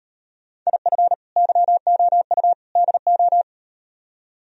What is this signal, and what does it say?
Morse code, heard as a keyed tone, about 38 wpm, 700 Hz. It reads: IF YOU DO